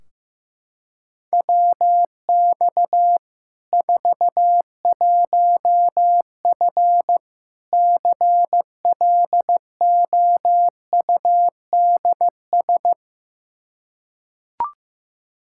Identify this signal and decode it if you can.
Morse code, heard as a keyed tone, about 15 wpm, 700 Hz. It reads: WX 41F CLOUDS
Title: Practice QSO - Exchange 2 - Info - 15wpm